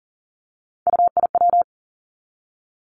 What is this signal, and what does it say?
Morse code, heard as a keyed tone, about 41 wpm, 700 Hz. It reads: UIW
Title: Three Challenging Letter-Number Combinations 40wpm